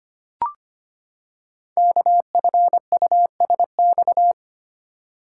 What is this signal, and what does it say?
Morse code, heard as a keyed tone, about 25 wpm, 700 Hz. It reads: KFUSX
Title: Code Groups - 5 Letters - 25wpm